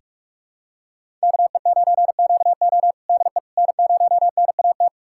Morse code, heard as a keyed tone, about 41 wpm, 700 Hz. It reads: KE9QO DE N0NAT
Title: USA Callsign DE Callsign - 5 Characters - 45wpm